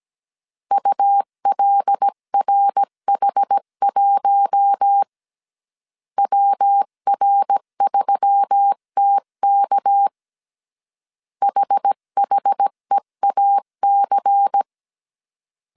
Morse code, heard as a keyed tone, about 17 wpm, 800 Hz. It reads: ULRH1 WR3TK HHEAC
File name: CW / DET / 1 6 1500C S C 16wpm 2733